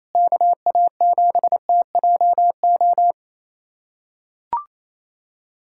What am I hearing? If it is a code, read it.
Morse code, heard as a keyed tone, about 28 wpm, 700 Hz. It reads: KA7TJO
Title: USA Callsign DE Callsign - 6 Characters - 28wpm